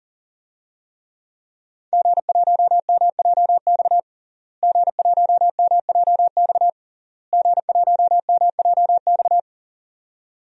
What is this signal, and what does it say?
Morse code, heard as a keyed tone, about 37 wpm, 700 Hz. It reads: G1MJX G1MJX G1MJX
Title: DXCC-100 Call Signs - 5 Characters - Repeated 3x - 40wpm